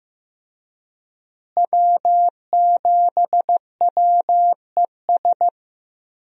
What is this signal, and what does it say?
Morse code, heard as a keyed tone, about 15 wpm, 700 Hz. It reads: W7WES
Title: USA Call Signs - 5 Characters - Repeated 2x - 15wpm